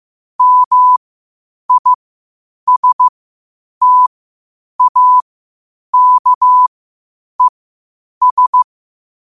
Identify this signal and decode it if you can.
Morse code, heard as a keyed tone, about 15 wpm, spaced as for 8 wpm, 1000 Hz. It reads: MISTAKES
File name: qotdfiles / 8wpm